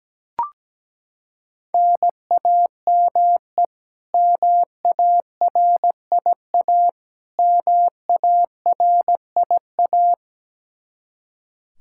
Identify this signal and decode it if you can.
Morse code, heard as a keyed tone, about 17 wpm, 700 Hz. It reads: NAME MARIA MARIA
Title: Practice QSO - Exchange 1 - Beginning - 17wpm